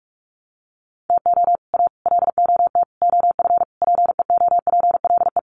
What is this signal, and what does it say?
Morse code, heard as a keyed tone, about 44 wpm, 700 Hz. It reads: TO A LOT OF PEOPLE
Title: Sets of 5 Words Encore 45wpm